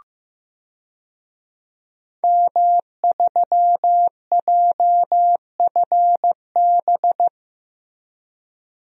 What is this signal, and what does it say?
Morse code, heard as a keyed tone, about 15 wpm, 700 Hz. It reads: M3JFB
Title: DXCC-100 Call Signs - 5 Characters - Repeated 2x - 15wpm